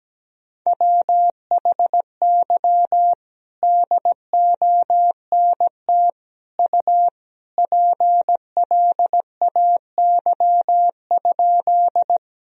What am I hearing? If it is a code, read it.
Morse code, heard as a keyed tone, about 17 wpm, 700 Hz. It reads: WHY DONT U PLAY?
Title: Sentences from Top 300 Words and Abbreviations 17wpm